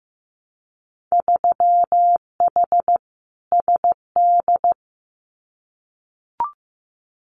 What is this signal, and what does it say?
Morse code, heard as a keyed tone, about 15 wpm, 700 Hz. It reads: 3H SD
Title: Winter Field Day Exchanges Encore 15wpm